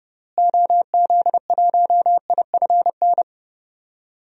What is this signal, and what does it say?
Morse code, heard as a keyed tone, about 32 wpm, 700 Hz. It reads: OZ1IFN
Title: DXCC-100 Call Signs - 6 Characters - Repeated 3x - 30wpm